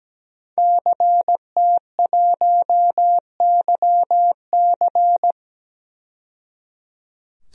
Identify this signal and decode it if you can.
Morse code, heard as a keyed tone, about 17 wpm, 700 Hz. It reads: CT1YC
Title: DXCC-100 Call Signs - 5 Characters - 17wpm